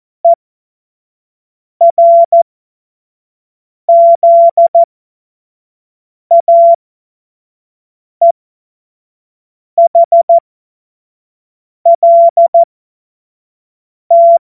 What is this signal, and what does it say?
Morse code, heard as a keyed tone, about 14 wpm, spaced as for 5 wpm, 680 Hz. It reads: ERZAEHLT
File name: Morsekurs / Witze / Witz 11. Juli 2023 25 F70